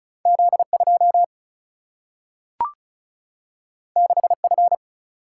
Morse code, heard as a keyed tone, about 36 wpm, 700 Hz. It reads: Z2 6F
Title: Single-Letter Single-Number 35wpm